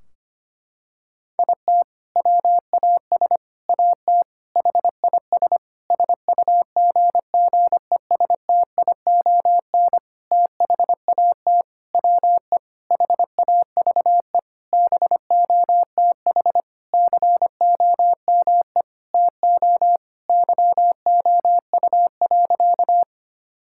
Morse code, heard as a keyed tone, about 25 wpm, 700 Hz. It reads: IT WAS AT HIS SUGGESTION THAT WE HAVE BOTH COME TO YOU.
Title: Sentences from Top 1600 Words 25wpm